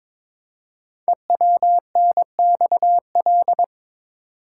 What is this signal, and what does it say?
Morse code, heard as a keyed tone, about 22 wpm, 700 Hz. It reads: EWNXL
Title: Code Groups - 5 Letters - 22wpm